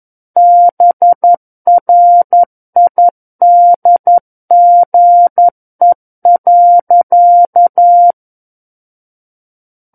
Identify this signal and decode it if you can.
Morse code, heard as a keyed tone, about 11 wpm, 700 Hz. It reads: BRIDGE.
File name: Book One / 02-Chapter 2 - 11 WPM